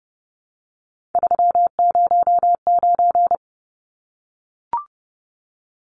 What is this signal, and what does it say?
Morse code, heard as a keyed tone, about 30 wpm, 700 Hz. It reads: 309
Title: Numbers - 3-Digit 30wpm